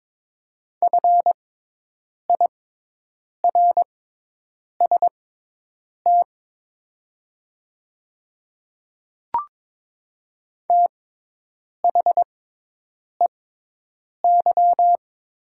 Morse code, heard as a keyed tone, about 22 wpm, spaced as for 8 wpm, 700 Hz. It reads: FIRST THEY
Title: 6x Character Spacing - Top 100 Words - 22wpm